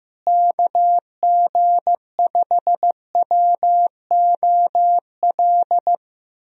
Morse code, heard as a keyed tone, about 15 wpm, 700 Hz. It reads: KG5WOL
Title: USA Call Signs - 6 Characters - Repeated 3x - 15wpm